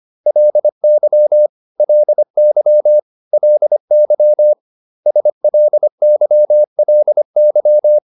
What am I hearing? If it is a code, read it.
Morse code, heard as a keyed tone, about 25 wpm, 600 Hz. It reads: LY LY LY SLYLY